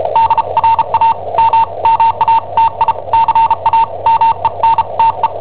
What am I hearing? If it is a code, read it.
Morse code, heard as a keyed tone, about 34 wpm, spaced as for 28 wpm, 910 Hz. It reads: DRAMMATICAMENTE
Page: Cassetta esercizi preparata da I7OHP per IW5ELP